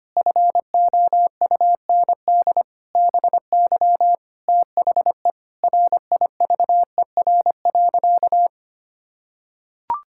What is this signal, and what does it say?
Morse code, heard as a keyed tone, about 25 wpm, 700 Hz. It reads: FOUND BY THE RIVER.
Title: Sentences from Top 1600 Words 25wpm